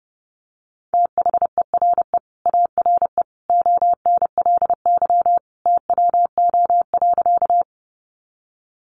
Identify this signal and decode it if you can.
Morse code, heard as a keyed tone, about 30 wpm, 700 Hz. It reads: THERE ARE ONLY TWO.
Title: Sentences from Top 200 Words 30wpm